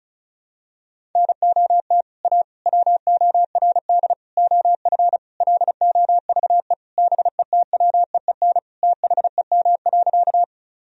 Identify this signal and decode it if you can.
Morse code, heard as a keyed tone, about 34 wpm, 700 Hz. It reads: NOT A WORD OF LOVE BETWEEN THEM.
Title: Sentences from Top 1000 Words - Review Part 1 - 35wpm